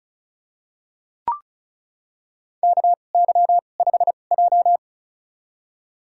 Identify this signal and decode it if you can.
Morse code, heard as a keyed tone, about 36 wpm, spaced as for 25 wpm, 700 Hz. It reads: KY5J
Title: USA Call Signs - 4 Characters - 2x Character Spacing - 35wpm